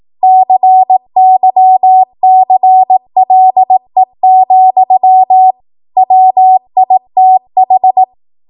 Morse code, heard as a keyed tone, about 18 wpm, 750 Hz. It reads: CYCLE, WITH
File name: Morse / 18WPM